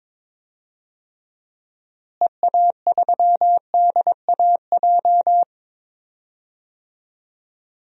Morse code, heard as a keyed tone, about 22 wpm, 700 Hz. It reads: EA3DAJ
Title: DXCC-100 Call Signs - 6 Characters - Repeated 3x - 22wpm